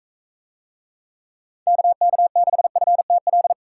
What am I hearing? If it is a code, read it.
Morse code, heard as a keyed tone, about 41 wpm, 700 Hz. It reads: KK6FTL